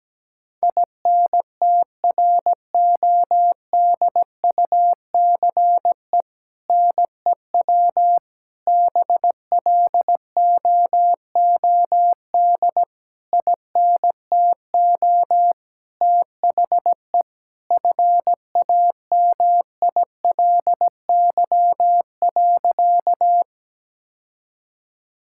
Morse code, heard as a keyed tone, about 17 wpm, 700 Hz. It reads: INTRODUCE NEW BLOOD INTO THE FAMILY.
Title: Sentences from Top 1600 Words 17wpm